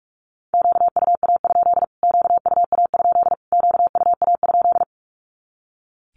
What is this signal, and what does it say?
Morse code, heard as a keyed tone, about 44 wpm, 700 Hz. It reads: QUA? QUA? QUA?
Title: Q-Codes Repeated 3x 45wpm